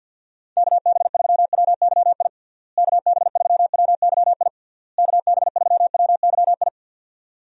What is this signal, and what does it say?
Morse code, heard as a keyed tone, about 46 wpm, 700 Hz. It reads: KB3WYI KB3WYI KB3WYI
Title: USA Call Signs - 6 Characters - Repeated 3x - 50wpm